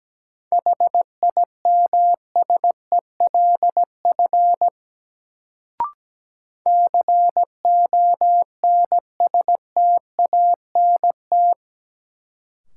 Morse code, heard as a keyed tone, about 17 wpm, 700 Hz. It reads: HIMSELF CONSTANT
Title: Sentences from Top 1500 Words - New Word Drill Alt - 17wpm